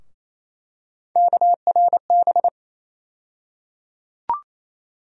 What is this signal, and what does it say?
Morse code, heard as a keyed tone, about 28 wpm, 700 Hz. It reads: KRB
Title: USA Call Sign Postfixes - 3 Letters 28wpm